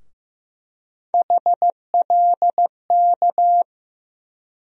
Morse code, heard as a keyed tone, about 15 wpm, 700 Hz. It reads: HLK
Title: USA Call Sign Postfixes - 3 Letters 15wpm